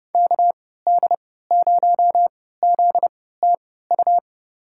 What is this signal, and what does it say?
Morse code, heard as a keyed tone, about 32 wpm, spaced as for 17 wpm, 700 Hz. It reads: KD0ZTU
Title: USA Call Signs - 3x Character Spacing - 30wpm